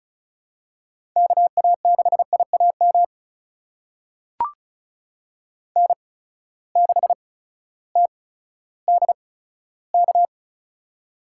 Morse code, heard as a keyed tone, about 36 wpm, spaced as for 16 wpm, 700 Hz. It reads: KA6IAM N6TDK
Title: USA Call Signs - 8x Character Spacing - 35wpm